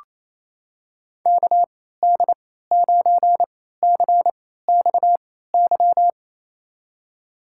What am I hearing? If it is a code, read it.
Morse code, heard as a keyed tone, about 28 wpm, spaced as for 16 wpm, 700 Hz. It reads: KD9CXY
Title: USA Call Signs - 6 Characters - 3x Character Spacing - 28wpm